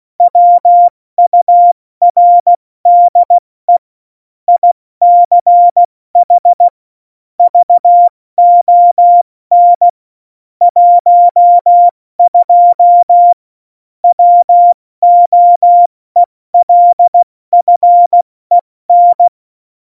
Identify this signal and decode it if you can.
Morse code, heard as a keyed tone, about 16 wpm, 700 Hz. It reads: WURDE ICH VON 12 WOELFEN
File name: Morsekurs / Witze / Witz 12. September 17, Tempo 70